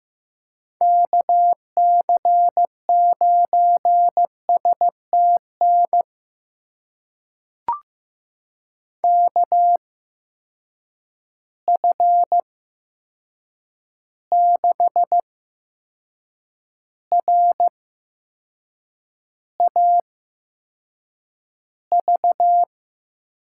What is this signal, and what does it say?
Morse code, heard as a keyed tone, about 15 wpm, spaced as for 6 wpm, 700 Hz. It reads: KC9STN KF6RAV
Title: USA Call Signs - 6 Characters - 8x Character Spacing - 15wpm